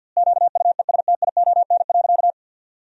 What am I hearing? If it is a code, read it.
Morse code, heard as a keyed tone, about 50 wpm, 700 Hz. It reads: QUESTION.
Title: 2x Word-Spacing - Sentences from Top 400 Words - 50wpm